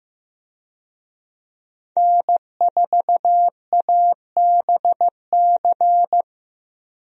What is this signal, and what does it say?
Morse code, heard as a keyed tone, about 15 wpm, 700 Hz. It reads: N4ABC
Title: USA Call Signs - 5 Characters - Repeated 3x - 15wpm